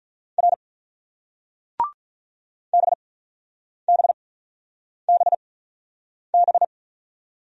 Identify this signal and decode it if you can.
Morse code, heard as a keyed tone, about 47 wpm, spaced as for 9 wpm, 700 Hz. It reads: R BBBB